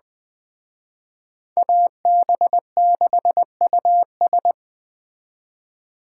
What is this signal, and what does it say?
Morse code, heard as a keyed tone, about 20 wpm, 700 Hz. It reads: AB6US